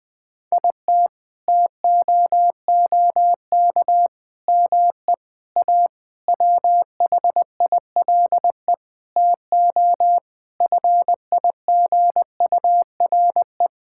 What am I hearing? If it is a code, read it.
Morse code, heard as a keyed tone, about 20 wpm, 700 Hz. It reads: IT TOOK ME A WHILE TO FIGURE
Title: Sentences with 8-Word Phrases - 20wpm